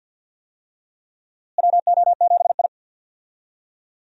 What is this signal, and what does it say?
Morse code, heard as a keyed tone, about 45 wpm, 700 Hz. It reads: WO7I
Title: USA Call Signs - 4 Characters - Repeated 3x - 50wpm